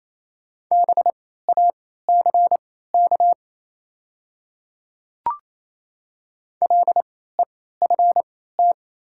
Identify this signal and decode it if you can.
Morse code, heard as a keyed tone, about 28 wpm, spaced as for 16 wpm, 700 Hz. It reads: BACK LEFT
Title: 3x Character Spacing - Top 300 Words - 28wpm